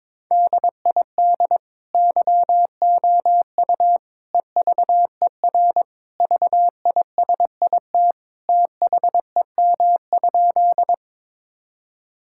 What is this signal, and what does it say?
Morse code, heard as a keyed tone, about 22 wpm, 700 Hz. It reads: DID YOU EVER VISIT THEM?
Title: Sentences from Top 1000 Words - Review Part 6 - 22wpm